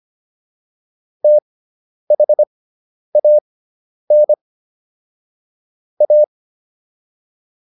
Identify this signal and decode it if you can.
Morse code, heard as a keyed tone, about 25 wpm, spaced as for 10 wpm, 600 Hz. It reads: THAN A